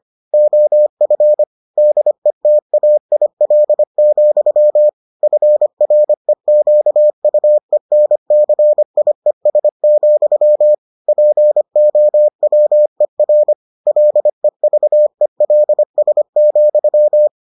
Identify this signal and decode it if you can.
Morse code, heard as a keyed tone, about 25 wpm, 600 Hz. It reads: OF DETAIL, FREQUENCIES, POWER LEVELS,